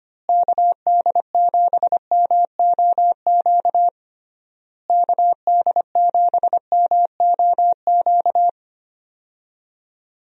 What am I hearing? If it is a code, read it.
Morse code, heard as a keyed tone, about 25 wpm, 700 Hz. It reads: KD7MOQ KD7MOQ
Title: USA Call Signs - 6 Characters - Repeated 3x - 25wpm